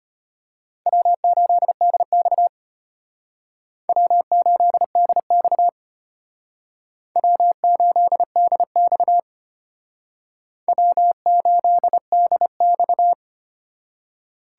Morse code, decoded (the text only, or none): W8DX W8DX W8DX W8DX